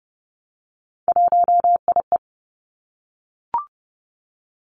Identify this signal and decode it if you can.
Morse code, heard as a keyed tone, about 30 wpm, 700 Hz. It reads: 1IE
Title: Three Challenging Letter-Number Combinations 30wpm